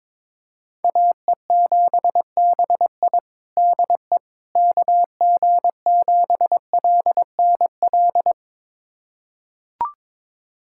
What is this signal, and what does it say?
Morse code, heard as a keyed tone, about 22 wpm, 700 Hz. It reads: AE7BI DE KG7LNL
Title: USA Callsign DE Callsign 22wpm